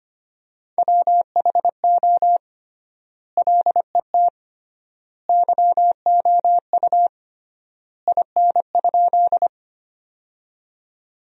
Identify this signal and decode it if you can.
Morse code, heard as a keyed tone, about 25 wpm, 700 Hz. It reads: WHO LET YOU IN?
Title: 3x Word-Spacing - Sentences from Top 300 Words - 25wpm